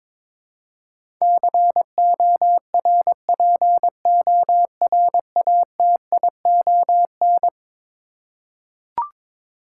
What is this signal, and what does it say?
Morse code, heard as a keyed tone, about 21 wpm, 700 Hz. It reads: CORPORATION
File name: practice / 22wpm / Eleven Letter Words - 22wpm